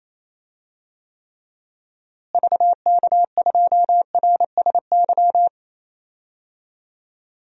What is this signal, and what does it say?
Morse code, heard as a keyed tone, about 28 wpm, 700 Hz. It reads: VK2RSY
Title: DXCC-100 Call Signs - 6 Characters - Repeated 2x - 28wpm